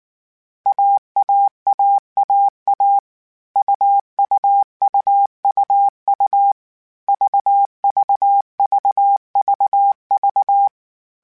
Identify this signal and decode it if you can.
Morse code, heard as a keyed tone, about 19 wpm, 800 Hz. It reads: AAAAA UUUUU VVVVV
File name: Difficult Combos / Difficult Combos-mp3s / AUV AUV4 @18